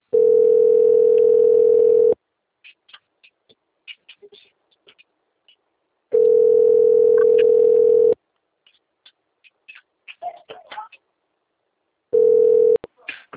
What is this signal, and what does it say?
Ringback tone (North America), 440 + 480 Hz.